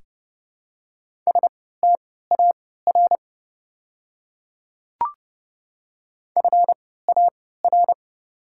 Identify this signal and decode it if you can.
Morse code, heard as a keyed tone, about 30 wpm, spaced as for 17 wpm, 700 Hz. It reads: STAR FAR
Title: 3x Character Spacing - Top 500 Words - 30wpm